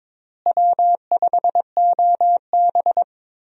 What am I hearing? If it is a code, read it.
Morse code, heard as a keyed tone, about 21 wpm, 700 Hz. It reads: W5OB
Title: USA Call Signs - 4 Characters - Repeated 2x - 22wpm